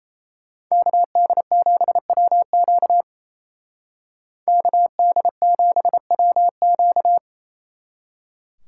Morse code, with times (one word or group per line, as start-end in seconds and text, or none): 0.7-3.0 KD7WQ
4.5-7.2 KD7WQ